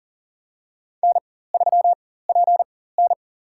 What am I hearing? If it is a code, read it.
Morse code, heard as a keyed tone, about 40 wpm, spaced as for 19 wpm, 700 Hz. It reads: N3PN
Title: USA Call Signs - 4x Character Spacing - 40wpm